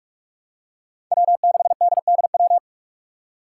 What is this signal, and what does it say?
Morse code, heard as a keyed tone, about 46 wpm, 700 Hz. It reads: W6DDW
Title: Five Challenging Letter-Number Combinations 45wpm